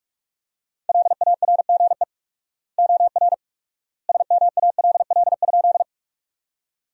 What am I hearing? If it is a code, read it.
Morse code, heard as a keyed tone, about 42 wpm, 700 Hz. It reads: LARGE OR SMALL?
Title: 4x Word-Spacing - Sentences from Top 200 Words - 45wpm